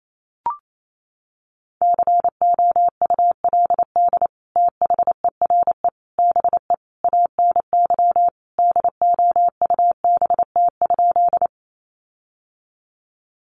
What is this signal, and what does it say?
Morse code, heard as a keyed tone, about 28 wpm, 700 Hz. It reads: COULD THERE BE ANY DOUBT?
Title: Sentences from Top 1500 Words 28wpm